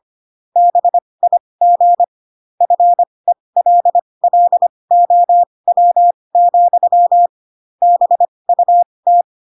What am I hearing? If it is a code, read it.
Morse code, heard as a keyed tone, about 25 wpm, spaced as for 20 wpm, 700 Hz. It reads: BIG FELLOW, BUT